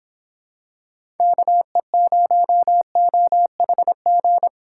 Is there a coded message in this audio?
Yes: Morse code from 1.2 to 4.5 s, heard as a keyed tone, about 26 wpm, 700 Hz. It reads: KE0OHG